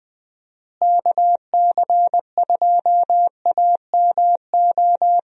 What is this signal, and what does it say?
Morse code, heard as a keyed tone, about 20 wpm, 700 Hz. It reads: KC2AMO
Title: USA Call Signs - 6 Characters - Repeated 2x - 20wpm